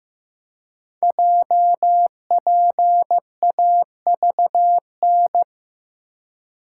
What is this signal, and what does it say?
Morse code, heard as a keyed tone, about 15 wpm, 700 Hz. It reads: JPAVN
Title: Code Groups - 5 Letters - 15wpm